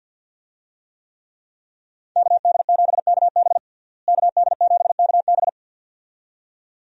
Morse code, heard as a keyed tone, about 52 wpm, 700 Hz. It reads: KD7KB KD7KB
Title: USA Call Signs - 5 Characters - Repeated 2x - 50wpm